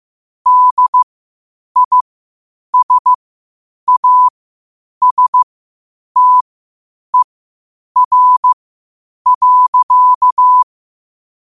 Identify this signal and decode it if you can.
Morse code, heard as a keyed tone, about 15 wpm, spaced as for 8 wpm, 1000 Hz. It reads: DISASTER.
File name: qotdfiles / 8wpm